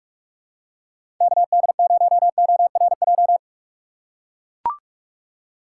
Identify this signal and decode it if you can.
Morse code, heard as a keyed tone, about 42 wpm, 700 Hz. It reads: KD0ORJ
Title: USA Call Signs - 6 Characters - 2x Character Spacing - 45wpm